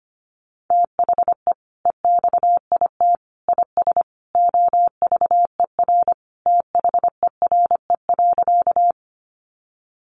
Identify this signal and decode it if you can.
Morse code, heard as a keyed tone, about 25 wpm, 700 Hz. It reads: THE EXIT IS OVER THERE.
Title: Sentences from Top 1100 Words 25wpm